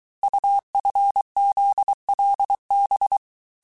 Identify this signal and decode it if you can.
Morse code, heard as a keyed tone, about 23 wpm, 800 Hz. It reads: UFZLB
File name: cours-du-mercredi / audios / cours-855-texte-chiffre-vitesse-22-mpm